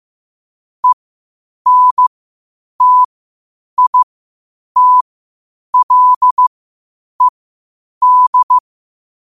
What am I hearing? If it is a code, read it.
Morse code, heard as a keyed tone, about 15 wpm, spaced as for 8 wpm, 1000 Hz. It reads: ENTITLED
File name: qotdfiles / 8wpm